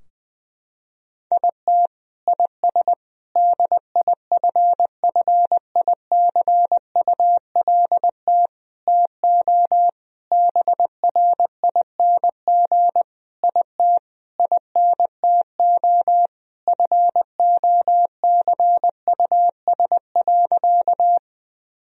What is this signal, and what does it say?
Morse code, heard as a keyed tone, about 20 wpm, 700 Hz. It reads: IT IS DIFFICULT TO BRING IT INTO FOCUS.
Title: Sentences from Top 1000 Words - Review Part 4 - 20wpm